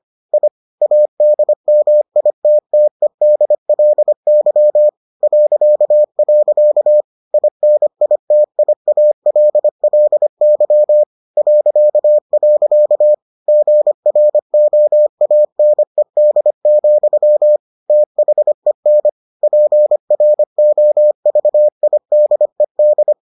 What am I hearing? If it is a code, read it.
Morse code, heard as a keyed tone, about 25 wpm, 600 Hz. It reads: I ADMITTEDLY .. INITIALLY .. GROANED, THEN PROVIDED